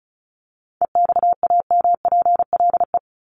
Morse code, heard as a keyed tone, about 34 wpm, 700 Hz. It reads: EXAMPLE
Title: Top 300 Words 35wpm